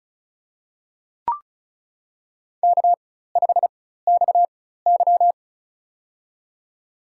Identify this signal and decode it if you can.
Morse code, heard as a keyed tone, about 34 wpm, spaced as for 16 wpm, 700 Hz. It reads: K5XY
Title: USA Call Signs - 4 Characters - 4x Character Spacing - 35wpm